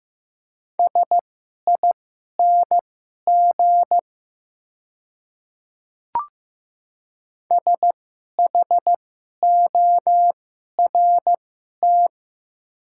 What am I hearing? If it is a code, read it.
Morse code, heard as a keyed tone, about 15 wpm, spaced as for 11 wpm, 700 Hz. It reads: SING SHORT
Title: 2x Character Spacing - Top 500 Words - 15wpm